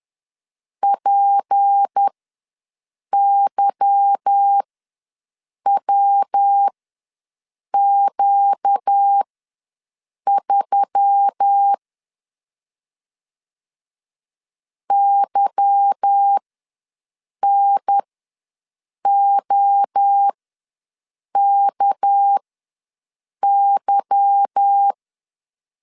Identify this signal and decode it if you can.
Morse code, heard as a keyed tone, about 10 wpm, spaced as for 6 wpm, 800 Hz. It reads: PYWQ3 YNOKY
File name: LEZIONI / 8 YJ45 / 8 Y ws 10wpm 3952